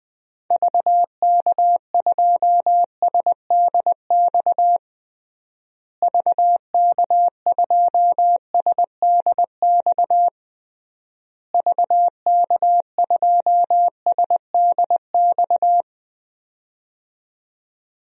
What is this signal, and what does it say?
Morse code, heard as a keyed tone, about 20 wpm, 700 Hz. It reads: VK2SDX VK2SDX VK2SDX
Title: DXCC-100 Call Signs - 6 Characters - Repeated 3x - 20wpm